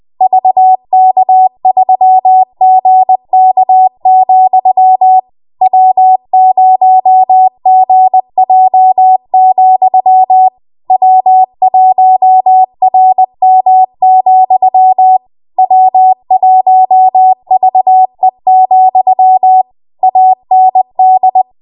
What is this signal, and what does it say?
Morse code, heard as a keyed tone, about 20 wpm, 750 Hz. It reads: VK3GK, W0GJ, W1RM, W1VE, AND
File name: Morse / 20WPM